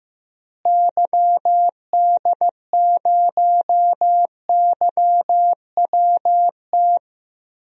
Morse code, heard as a keyed tone, about 15 wpm, 700 Hz. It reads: YD0YWT
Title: DXCC-100 Call Signs - 6 Characters - Repeated 3x - 15wpm